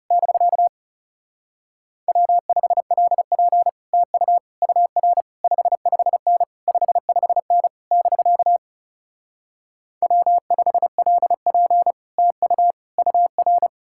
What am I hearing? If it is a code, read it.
Morse code, heard as a keyed tone, about 40 wpm, 700 Hz. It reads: <BK> W5LP TU UR 55N 55N <BK> W5LP TU UR